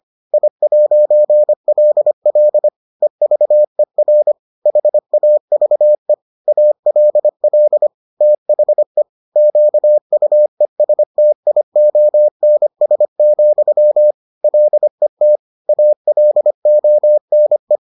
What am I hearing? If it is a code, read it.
Morse code, heard as a keyed tone, about 25 wpm, 600 Hz. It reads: I'LL EVER HAVE ALL THE QUESTIONS, LET ALONE